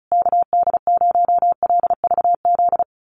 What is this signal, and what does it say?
Morse code, heard as a keyed tone, about 34 wpm, 700 Hz. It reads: KD0LVZ
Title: USA Call Signs Repeated 3x 35wpm